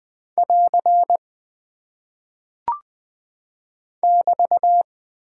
Morse code, heard as a keyed tone, about 20 wpm, 700 Hz. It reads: <AR> <BT>
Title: Single Punctuation-Prosign - 20wpm